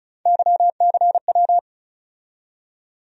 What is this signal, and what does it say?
Morse code, heard as a keyed tone, about 36 wpm, 700 Hz. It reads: YCW
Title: USA Call Sign Postfixes - 3 Letters 35wpm